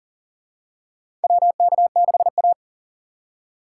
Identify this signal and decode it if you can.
Morse code, heard as a keyed tone, about 40 wpm, 700 Hz. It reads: WK6A
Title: USA Call Signs - 4x Character Spacing - 40wpm